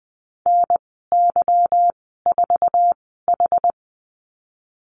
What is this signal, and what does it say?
Morse code, heard as a keyed tone, about 20 wpm, spaced as for 14 wpm, 700 Hz. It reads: NY4H
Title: 213 Lesson 34g - Number 8 - Call Signs 20wpm